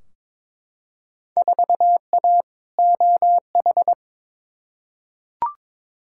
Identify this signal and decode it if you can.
Morse code, heard as a keyed tone, about 22 wpm, 700 Hz. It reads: 4A OH
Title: Field Day Exchanges - 22wpm